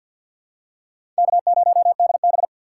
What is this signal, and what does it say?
Morse code, heard as a keyed tone, about 47 wpm, 700 Hz. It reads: K0DB
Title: USA Call Signs - 2x Character Spacing - 50wpm